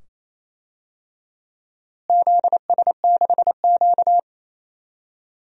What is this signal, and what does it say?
Morse code, heard as a keyed tone, about 28 wpm, 700 Hz. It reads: ZS6Q
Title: DXCC-100 Call Signs - 4 Characters - Repeated 2x - 28wpm